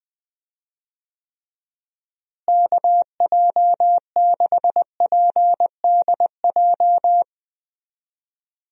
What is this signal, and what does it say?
Morse code, heard as a keyed tone, about 21 wpm, 700 Hz. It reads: KJ6PDJ
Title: USA Call Signs - 6 Characters - Repeated 2x - 20wpm